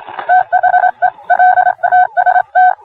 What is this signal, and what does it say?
Morse code, heard as a keyed tone, about 26 wpm, 730 Hz. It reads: THE L TST